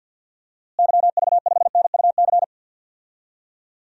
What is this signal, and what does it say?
Morse code, heard as a keyed tone, about 46 wpm, 700 Hz. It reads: YV5NUC
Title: DXCC-100 Call Signs - 6 Characters - Repeated 2x - 50wpm